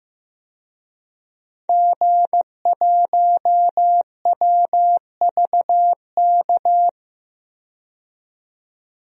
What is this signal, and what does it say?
Morse code, heard as a keyed tone, about 15 wpm, 700 Hz. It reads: G1WVK
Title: DXCC-100 Call Signs - 5 Characters - 15wpm